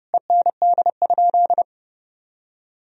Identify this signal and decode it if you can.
Morse code, heard as a keyed tone, about 31 wpm, 700 Hz. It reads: END?
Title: Sentences from Top 1000 Words - Review Part 4 - 30wpm